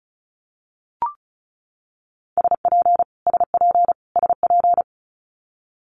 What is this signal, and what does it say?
Morse code, heard as a keyed tone, about 34 wpm, 700 Hz. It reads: SP SP SP
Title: Multi-letter Phonemes Repeated 3x 35wpm